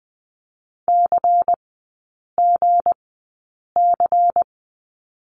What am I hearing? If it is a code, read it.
Morse code, heard as a keyed tone, about 20 wpm, spaced as for 8 wpm, 700 Hz. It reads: CGC
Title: 184 Lesson 30e - Letter b - Three Characters 20wpm